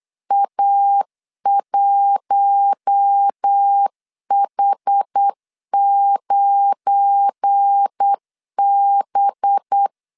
Morse code, heard as a keyed tone, about 8 wpm, 800 Hz. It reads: A1H9B
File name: CW / DET / 1 9 1500C S C 8wpm 5653